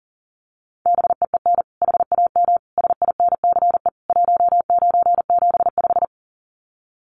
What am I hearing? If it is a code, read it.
Morse code, heard as a keyed tone, about 40 wpm, 700 Hz. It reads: BEEN HAM SINCE 1975
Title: Common QSO Phrases 40wpm